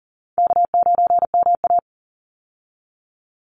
Morse code, heard as a keyed tone, about 40 wpm, 700 Hz. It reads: K9MA
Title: Easy Call Signs Repeated 3x - 40wpm